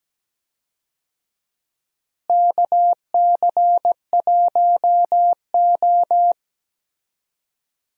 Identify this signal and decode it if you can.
Morse code, heard as a keyed tone, about 17 wpm, 700 Hz. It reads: KC1O
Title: USA Call Signs - 4 Characters - Repeated 2x - 17wpm